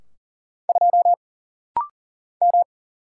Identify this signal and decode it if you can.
Morse code, heard as a keyed tone, about 40 wpm, 700 Hz. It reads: 2 M
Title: Single Letter-Number - Rapid-Fire 40wpm